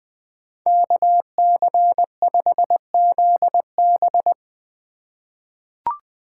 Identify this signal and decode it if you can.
Morse code, heard as a keyed tone, about 21 wpm, 700 Hz. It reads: KC5ZB
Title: USA Call Signs - 4x Character Spacing - 20wpm